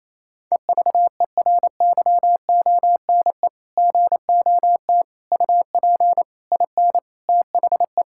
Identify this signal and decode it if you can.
Morse code, heard as a keyed tone, about 28 wpm, 700 Hz. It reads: EVERYONE GOT UP IN THE
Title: Sentences from Top 1000 Words - Review Part 5 - 28wpm